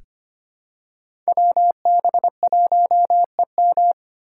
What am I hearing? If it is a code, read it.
Morse code, heard as a keyed tone, about 25 wpm, 690 Hz. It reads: WB1EM
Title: Code Groups - 5 Letter:Numbers - 25wpm